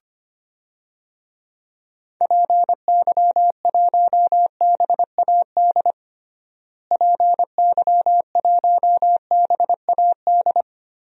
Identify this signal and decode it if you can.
Morse code, heard as a keyed tone, about 25 wpm, 700 Hz. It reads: PY1BAD PY1BAD
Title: DXCC-100 Call Signs - 6 Characters - Repeated 3x - 25wpm